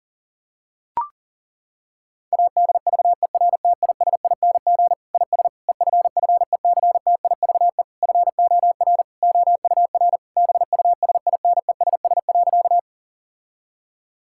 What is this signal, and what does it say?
Morse code, heard as a keyed tone, about 40 wpm, 700 Hz. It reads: ADVERTISING IS EFFECTIVE FOR OUR BUSINESS.
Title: Sentences from Top 1000 Words - Review Part 6 - 40wpm